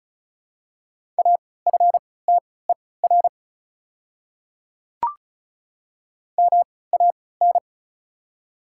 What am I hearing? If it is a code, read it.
Morse code, heard as a keyed tone, about 34 wpm, spaced as for 20 wpm, 700 Hz. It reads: AFTER MAN
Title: 3x Character Spacing - Top 300 Words - 35wpm